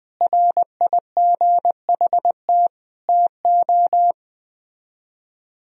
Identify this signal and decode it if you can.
Morse code, heard as a keyed tone, about 21 wpm, 700 Hz. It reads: RIGHT TO